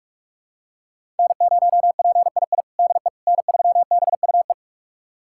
Morse code, heard as a keyed tone, about 46 wpm, 700 Hz. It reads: N0PII DE N3BUE
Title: USA Callsign DE Callsign - 5 Characters - 45wpm